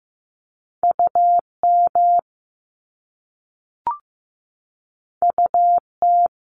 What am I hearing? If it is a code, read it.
Morse code, heard as a keyed tone, about 15 wpm, 700 Hz. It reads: UM UT
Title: Top Two Letter Combinations in Words 15wpm